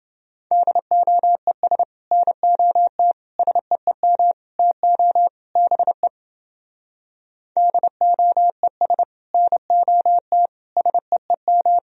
Morse code, heard as a keyed tone, about 28 wpm, 700 Hz. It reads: DOES NOT SEEM TO BE DOES NOT SEEM